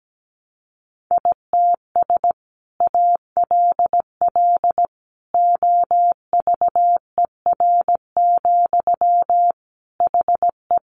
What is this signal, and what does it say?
Morse code, heard as a keyed tone, about 17 wpm, 700 Hz. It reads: ITS ALL OVER, HE
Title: Sentences from Top 600 Words 17wpm